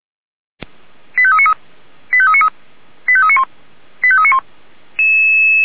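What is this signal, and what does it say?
Five-tone selective call (ZVEI), heard as quick runs of five tones, tones of about 70 ms. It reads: 85393 85393 85391 85391
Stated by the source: Five-tone ZVEI selective call 89484 received by SDR.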